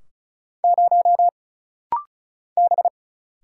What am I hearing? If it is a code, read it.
Morse code, heard as a keyed tone, about 36 wpm, 700 Hz. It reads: 0 B
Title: Single Letter-Number - Rapid-Fire 35wpm